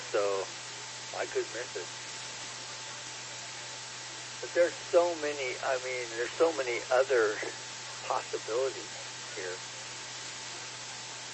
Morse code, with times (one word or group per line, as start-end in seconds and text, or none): none